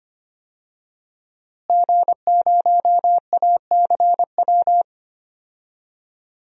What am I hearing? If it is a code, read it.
Morse code, heard as a keyed tone, about 25 wpm, 700 Hz. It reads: G0ACW
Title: DXCC-100 Call Signs - 5 Characters - Repeated 3x - 25wpm